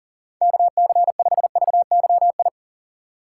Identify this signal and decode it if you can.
Morse code, heard as a keyed tone, about 40 wpm, 700 Hz. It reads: KC5VYI
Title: USA Call Signs - 6 Characters - 3x Character Spacing - 40wpm